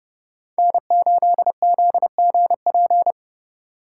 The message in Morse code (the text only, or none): N8ZGP